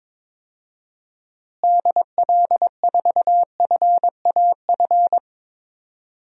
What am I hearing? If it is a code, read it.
Morse code, heard as a keyed tone, about 21 wpm, 700 Hz. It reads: DL4FAF
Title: DXCC-100 Call Signs - 6 Characters - Repeated 3x - 22wpm